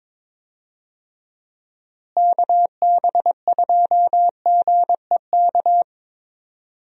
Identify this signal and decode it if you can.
Morse code, heard as a keyed tone, about 22 wpm, 700 Hz. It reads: KB2GEK
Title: USA Call Signs - 6 Characters - Repeated 2x - 22wpm